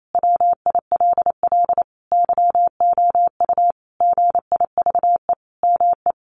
Morse code, heard as a keyed tone, about 28 wpm, 700 Hz. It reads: WILL YOU GIVE ME
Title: Sentences from Top 1200 Words 28wpm